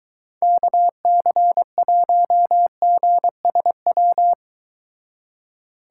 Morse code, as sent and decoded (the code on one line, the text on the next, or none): -.- -.-. .---- --. ... .--
KC1GSW